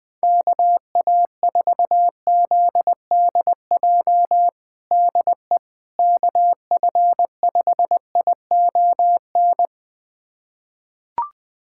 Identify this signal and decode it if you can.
Morse code, heard as a keyed tone, about 20 wpm, 700 Hz. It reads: KA4ZDJ DE KF5ION
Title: USA Callsign DE Callsign - 6 Characters - 20wpm